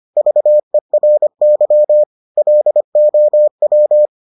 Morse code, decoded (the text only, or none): VERY LOW